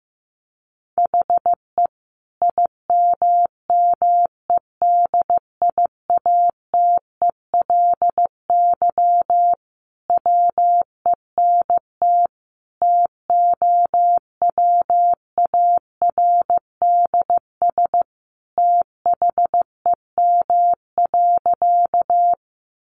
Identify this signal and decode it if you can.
Morse code, heard as a keyed tone, about 15 wpm, 700 Hz. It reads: HE IMMEDIATELY WENT TOWARDS THEM.
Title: Sentences from Top 1100 Words 15wpm